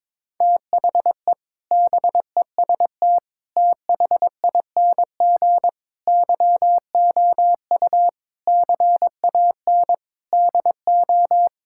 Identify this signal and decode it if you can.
Morse code, heard as a keyed tone, about 22 wpm, 700 Hz. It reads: THE BEST THING YOU CAN DO
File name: practice / 22wpm / Sets of 7 Words - 22wpm